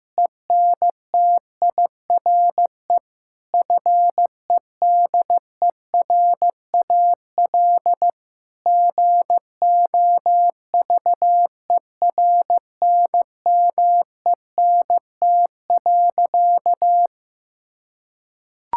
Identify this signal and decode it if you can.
Morse code, heard as a keyed tone, about 15 wpm, 700 Hz. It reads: ENTIRE FEDERAL GOVERNMENT.
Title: Sentences from Top 1000 Words - Review Part 5 - 15wpm